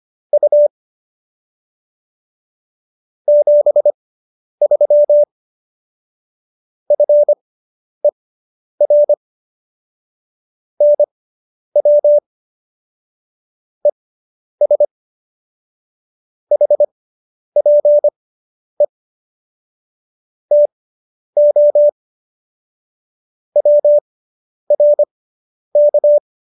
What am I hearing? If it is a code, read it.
Morse code, heard as a keyed tone, about 25 wpm, spaced as for 10 wpm, 590 Hz. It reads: U 73 FER NW ES HPE TO WRK
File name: QSO220 10